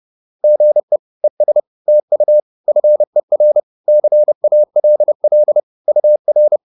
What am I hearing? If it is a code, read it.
Morse code, heard as a keyed tone, about 30 wpm, 600 Hz. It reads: GE ES TU FER CALL UR